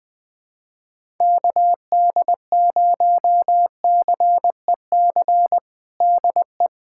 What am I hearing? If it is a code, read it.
Morse code, heard as a keyed tone, about 21 wpm, 700 Hz. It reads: KD0CEC DE
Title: USA Callsign DE Callsign - 6 Characters - 20wpm